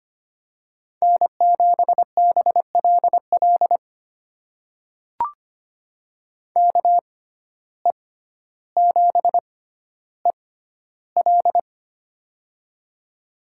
Morse code, heard as a keyed tone, about 25 wpm, spaced as for 13 wpm, 700 Hz. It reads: N7BLL KE7EL
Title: USA Call Signs - 5 Characters - 6x Character Spacing - 25wpm